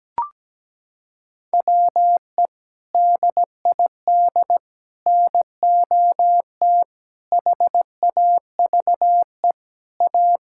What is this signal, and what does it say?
Morse code, heard as a keyed tone, about 17 wpm, 700 Hz. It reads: WE DID NOT HAVE A
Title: Sentences from Top 1600 Words 17wpm